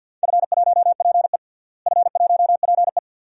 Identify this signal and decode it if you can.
Morse code, heard as a keyed tone, about 51 wpm, 700 Hz. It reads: F1PE F1PE
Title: DXCC-100 Call Signs - 4 Characters - Repeated 2x - 50wpm